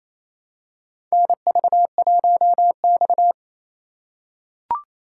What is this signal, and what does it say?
Morse code, heard as a keyed tone, about 27 wpm, 700 Hz. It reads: NV1X
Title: USA Call Signs - 4 Characters - 4x Character Spacing - 28wpm